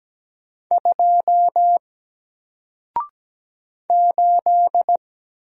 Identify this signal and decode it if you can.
Morse code, heard as a keyed tone, about 17 wpm, 700 Hz. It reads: 2 8
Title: Single Character (Letter:Number:Punctuation:Prosign) Alt 17wpm